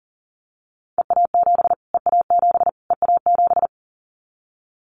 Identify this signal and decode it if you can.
Morse code, heard as a keyed tone, about 40 wpm, 700 Hz. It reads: EA7 EA7 EA7
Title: Callsign Prefixes for Top 25 DXCC - Repeated 3x - 40wpm